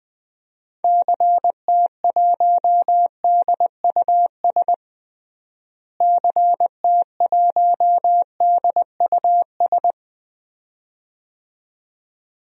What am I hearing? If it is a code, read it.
Morse code, heard as a keyed tone, about 21 wpm, 700 Hz. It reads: CT1DUS CT1DUS
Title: DXCC-100 Call Signs - 6 Characters - Repeated 2x - 20wpm